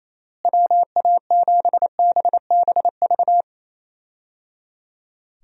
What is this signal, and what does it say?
Morse code, heard as a keyed tone, about 28 wpm, 700 Hz. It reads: WA7BBV
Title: USA Call Signs - 6 Characters - Repeated 3x - 28wpm